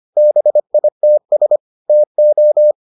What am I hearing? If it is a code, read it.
Morse code, heard as a keyed tone, about 25 wpm, 600 Hz. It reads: BITS TO